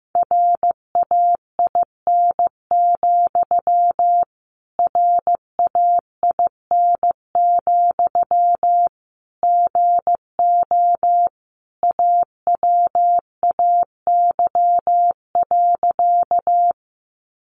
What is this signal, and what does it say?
Morse code, heard as a keyed tone, about 15 wpm, 700 Hz. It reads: RAIN, RAIN, GO AWAY.
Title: Sentences from Top 600 Words 15wpm